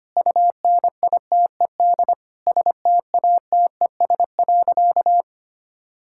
Morse code, heard as a keyed tone, about 25 wpm, 700 Hz. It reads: UNITED STATES.
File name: practice / 25wpm / Sentences with 4-Word Phrases - 25wpm